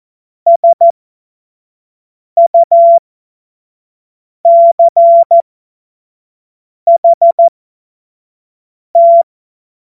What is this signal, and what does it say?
Morse code, heard as a keyed tone, about 14 wpm, spaced as for 5 wpm, 680 Hz. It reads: SUCHT
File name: Morsekurs